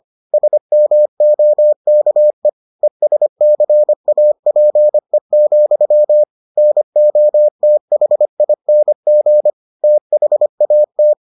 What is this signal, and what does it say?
Morse code, heard as a keyed tone, about 25 wpm, 600 Hz. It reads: SMOKE ESCAPE, NOTHING THAT